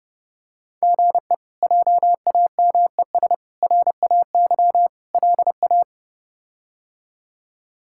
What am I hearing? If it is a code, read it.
Morse code, heard as a keyed tone, about 32 wpm, 700 Hz. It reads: GE JAMES RAY LA
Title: K1USN SST QSO Practice - 30wpm